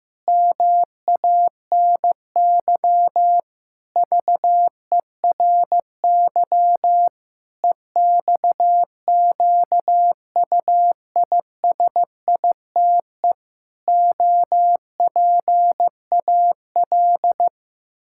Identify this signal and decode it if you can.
Morse code, heard as a keyed tone, about 15 wpm, 700 Hz. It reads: MANY VERY EXQUISITE OPAL
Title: Pangram Challenge v2 - 15wpm